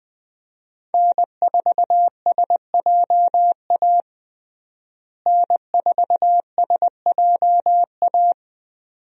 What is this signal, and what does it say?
Morse code, heard as a keyed tone, about 20 wpm, 700 Hz. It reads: N4SJA N4SJA
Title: USA Call Signs - 5 Characters - Repeated 2x - 20wpm